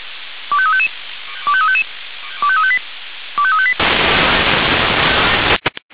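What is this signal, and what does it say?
Five-tone selective call (ZVEI), heard as quick runs of five tones, tones of about 70 ms. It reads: 25355 25355 25358 25358
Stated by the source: ZVEI five-tone selective call received on an SDR.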